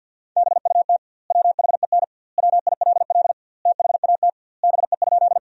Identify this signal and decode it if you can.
Morse code, heard as a keyed tone, about 50 wpm, 700 Hz. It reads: BUT WHEN WILL THAT BE?
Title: 2x Word-Spacing - Sentences from Top 100 Words - 50wpm